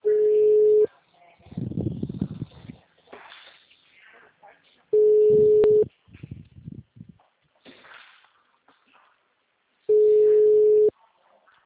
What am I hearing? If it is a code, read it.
Ringback tone (Europe, 425 Hz), with speech.
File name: out- -2024- - - .2153